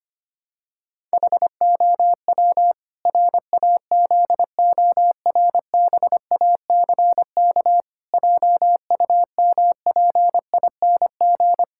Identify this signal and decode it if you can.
Morse code, heard as a keyed tone, about 25 wpm, 700 Hz. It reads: HOW RAZORBACK JUMPING
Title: Pangram Challenge v2 - 25wpm